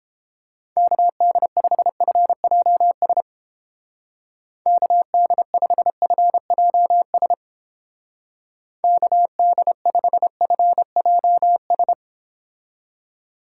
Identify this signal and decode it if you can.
Morse code, heard as a keyed tone, about 31 wpm, 700 Hz. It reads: KD5FJS KD5FJS KD5FJS